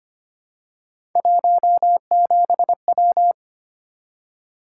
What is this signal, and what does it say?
Morse code, heard as a keyed tone, about 25 wpm, 700 Hz. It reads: 17W
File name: Three Letter-Number Combinations - 25wpm